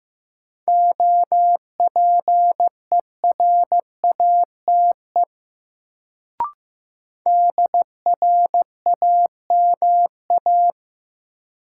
Morse code, heard as a keyed tone, about 15 wpm, 700 Hz. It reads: OPERATE DRAMA
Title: Sentences from Top 1500 Words - New Word Drill - 15wpm